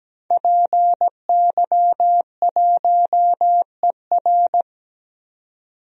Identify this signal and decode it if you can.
Morse code, heard as a keyed tone, about 17 wpm, 700 Hz. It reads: PY1ER
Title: DXCC-100 Call Signs - 5 Characters - 17wpm